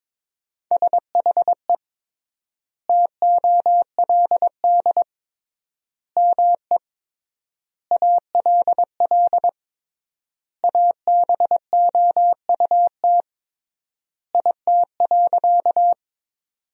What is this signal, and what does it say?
Morse code, heard as a keyed tone, about 22 wpm, 700 Hz. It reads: SHE TOLD ME ALL ABOUT IT.
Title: 3x Word-Spacing - Sentences from Top 400 Words - 22wpm